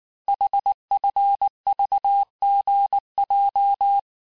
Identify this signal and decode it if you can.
Morse code, heard as a keyed tone, about 19 wpm, 800 Hz. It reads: HFVGJ
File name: cours-du-lundi files / cours-1583--texte-chiffre-18-mpm